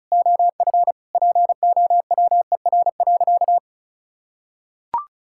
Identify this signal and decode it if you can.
Morse code, heard as a keyed tone, about 36 wpm, 700 Hz. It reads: OF POWER.
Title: Sentences from Top 1000 Words - Review Part 6 - 35wpm